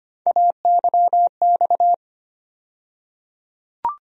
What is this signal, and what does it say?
Morse code, heard as a keyed tone, about 25 wpm, 700 Hz. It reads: AYX
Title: USA Call Sign Postfixes - 3 Letters 25wpm